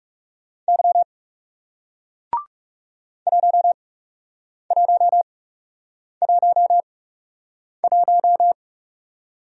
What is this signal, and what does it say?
Morse code, heard as a keyed tone, about 40 wpm, spaced as for 8 wpm, 700 Hz. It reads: Y 1111